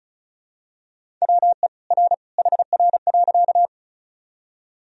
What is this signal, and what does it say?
Morse code, heard as a keyed tone, about 36 wpm, 700 Hz. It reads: WE R HR.
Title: Sentences from Top 200 Words and Abbreviations 35wpm